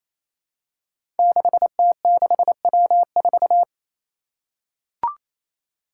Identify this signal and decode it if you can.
Morse code, heard as a keyed tone, about 28 wpm, 700 Hz. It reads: 6T6W4
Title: Five Challenging Letter-Number Combinations 28wpm